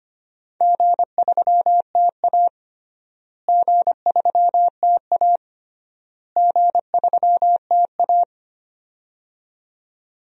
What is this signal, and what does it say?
Morse code, heard as a keyed tone, about 25 wpm, 700 Hz. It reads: G3TA G3TA G3TA
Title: DXCC-100 Call Signs - 4 Characters - Repeated 3x - 25wpm